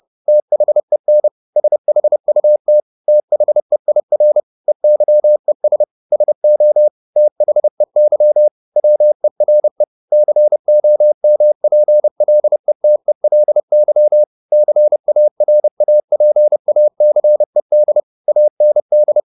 Morse code, heard as a keyed tone, about 30 wpm, 600 Hz. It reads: THEN SHUT THEIR EYES SO THEY WERE COMPLETELY CARAPACED AND
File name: the old man and the sea / the old man and the sea 1 30wpm